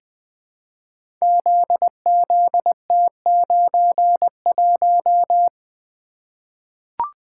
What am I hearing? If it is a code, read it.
Morse code, heard as a keyed tone, about 20 wpm, 700 Hz. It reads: ZZT91
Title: Code Groups - 5 Letter:Numbers - 20wpm